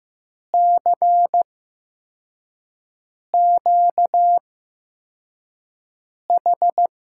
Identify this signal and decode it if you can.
Morse code, heard as a keyed tone, about 15 wpm, spaced as for 4 wpm, 700 Hz. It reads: CQH
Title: USA Call Signs - 5 Characters - 8x Character Spacing - 15wpm